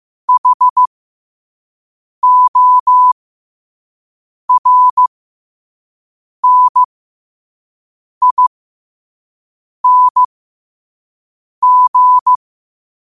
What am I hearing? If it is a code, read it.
Morse code, heard as a keyed tone, about 15 wpm, spaced as for 5 wpm, 1000 Hz. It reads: HORNING
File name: qotdfiles / 5wpm